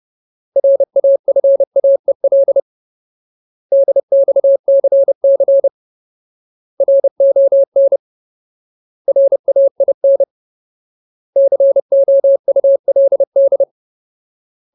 Morse code, heard as a keyed tone, about 30 wpm, 550 Hz. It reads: RAFAEL DXCC RON RAIN COULD
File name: Ultimate Word List / 30WPM Ultimate